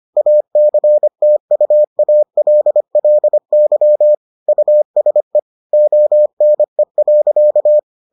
Morse code, heard as a keyed tone, about 25 wpm, 600 Hz. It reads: ACTUALLY USE ONE.